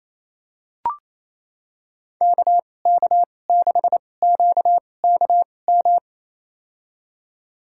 Morse code, heard as a keyed tone, about 28 wpm, spaced as for 20 wpm, 700 Hz. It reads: KK6QKM
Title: USA Call Signs - 6 Characters - 2x Character Spacing - 28wpm